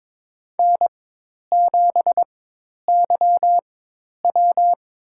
Morse code, heard as a keyed tone, about 22 wpm, spaced as for 10 wpm, 700 Hz. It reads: N7YW
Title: USA Call Signs - 4x Character Spacing - 22wpm